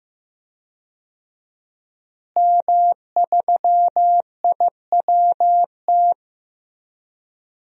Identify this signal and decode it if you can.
Morse code, heard as a keyed tone, about 15 wpm, 700 Hz. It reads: M3IWT
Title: DXCC-100 Call Signs - 5 Characters - Repeated 3x - 15wpm